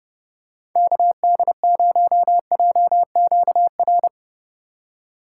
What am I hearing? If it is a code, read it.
Morse code, heard as a keyed tone, about 31 wpm, 700 Hz. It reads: KD0JQR